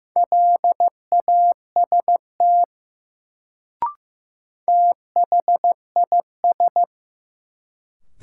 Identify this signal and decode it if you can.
Morse code, heard as a keyed tone, about 15 wpm, 700 Hz. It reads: LAST THIS
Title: Sets of 2 Words - New Word Drill - 15wpm